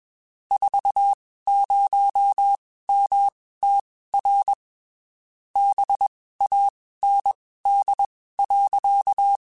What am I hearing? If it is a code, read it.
Morse code, heard as a keyed tone, about 21 wpm, spaced as for 15 wpm, 800 Hz. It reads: 40MTR BAND.